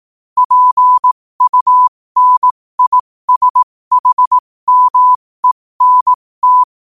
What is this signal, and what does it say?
Morse code, heard as a keyed tone, about 18 wpm, spaced as for 15 wpm, 1000 Hz. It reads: PUNISHMENT
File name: qotdfiles / 15wpm